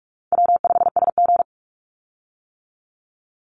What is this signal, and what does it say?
Morse code, heard as a keyed tone, about 44 wpm, 700 Hz. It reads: W5SG
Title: Call Signs - 45wpm